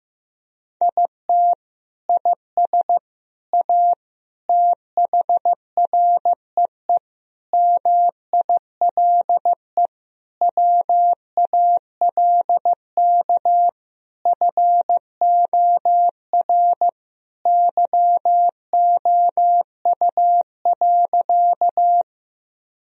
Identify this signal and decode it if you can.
Morse code, heard as a keyed tone, about 16 wpm, 700 Hz. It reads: IT IS A THREE MILE WALK FOR YOU.
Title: Sentences from Top 1000 Words - Review Part 1 - 15wpm